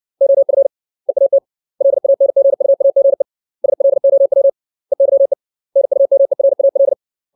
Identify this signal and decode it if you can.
Morse code, heard as a keyed tone, about 60 wpm, 550 Hz. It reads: OF EAT LANGUAGE SLOW EYE NUMERAL
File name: 632 words / 632 Words 60WPM